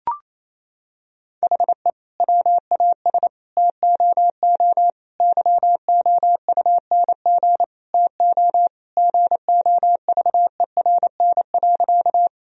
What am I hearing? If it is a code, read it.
Morse code, heard as a keyed tone, about 28 wpm, 700 Hz. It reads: HE WAS TOO YOUNG TO GOVERN.
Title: Sentences from Top 1000 Words - Review Part 2 - 28wpm